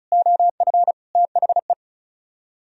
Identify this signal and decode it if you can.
Morse code, heard as a keyed tone, about 46 wpm, spaced as for 41 wpm, 700 Hz. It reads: OF THE
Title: Speed-Racing - Sets of 5 Words 35wpm